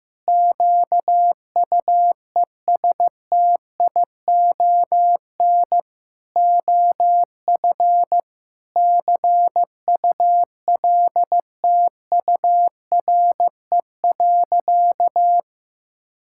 Morse code, heard as a keyed tone, about 15 wpm, 700 Hz. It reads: QUESTION OF CULTURE.
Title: Sentences from Top 1000 Words - Review Part 6 - 15wpm